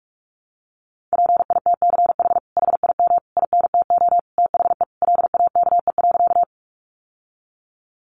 Morse code, heard as a keyed tone, about 46 wpm, 700 Hz. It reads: PITCH HIM INTO THE LAKE.
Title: Sentences from Top 1300 Words 45wpm